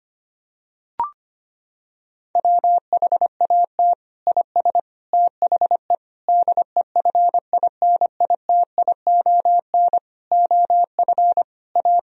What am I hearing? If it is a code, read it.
Morse code, heard as a keyed tone, about 25 wpm, 700 Hz. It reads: WHAT IS THE DEFINITION OF A
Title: Sentences from Top 1000 Words - Review Part 4 - 25wpm